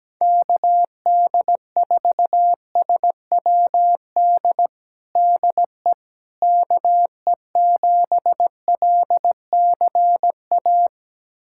Morse code, heard as a keyed tone, about 17 wpm, 700 Hz. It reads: KD4SWD DE KE7LCA
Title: USA Callsign DE Callsign - 6 Characters - 17wpm